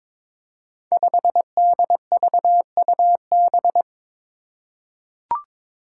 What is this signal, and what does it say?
Morse code, heard as a keyed tone, about 22 wpm, 700 Hz. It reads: 5DVUB
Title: Five Challenging Letter-Number Combinations 22wpm